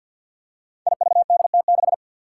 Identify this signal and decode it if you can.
Morse code, heard as a keyed tone, about 53 wpm, 700 Hz. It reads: IVDT6
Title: Five Challenging Letter-Number Combinations 50wpm